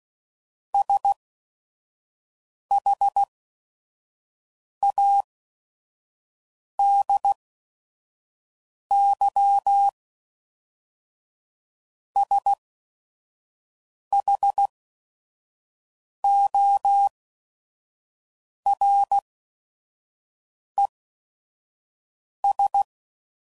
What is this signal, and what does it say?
Morse code, heard as a keyed tone, about 16 wpm, spaced as for 5 wpm, 800 Hz. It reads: SHADYSHORES
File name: CW Code Groups @5WPM / fromSM's / Cities